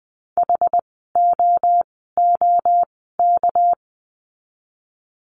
Morse code, heard as a keyed tone, about 20 wpm, spaced as for 14 wpm, 700 Hz. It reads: HOOK
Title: 199 Lesson 32f - Letter k - Words 20wpm